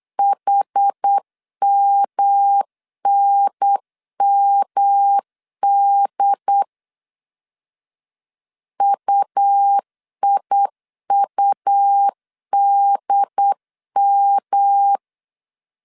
Morse code, heard as a keyed tone, about 8 wpm, 800 Hz. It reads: HMNMD UIUDM
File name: CW / DET / 1 3 1500C S B 8wpm 4141